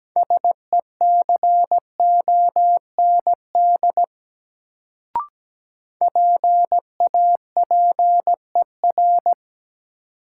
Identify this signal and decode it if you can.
Morse code, heard as a keyed tone, about 17 wpm, 700 Hz. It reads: SECOND PAPER
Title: Sentences from Top 300 Words - New Word Drill - 17wpm